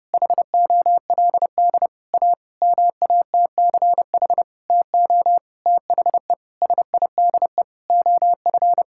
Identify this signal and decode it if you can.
Morse code, heard as a keyed tone, about 30 wpm, 700 Hz. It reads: HOLD A MATCH TO THE SIDE OF
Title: Sentences from Top 1600 Words 30wpm